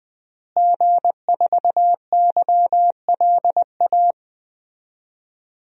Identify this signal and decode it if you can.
Morse code, heard as a keyed tone, about 20 wpm, 700 Hz. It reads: G4YLA
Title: DXCC-100 Call Signs - 5 Characters - Repeated 3x - 20wpm